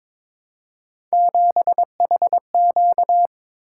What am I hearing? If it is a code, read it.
Morse code, heard as a keyed tone, about 21 wpm, 700 Hz. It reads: 7HQ
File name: practice / 22wpm / Three Letter-Number Combinations - 22wpm